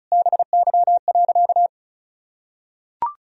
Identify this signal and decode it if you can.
Morse code, heard as a keyed tone, about 36 wpm, 700 Hz. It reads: BY.
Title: Sentences from Top 1000 Words - Review Part 4 - 35wpm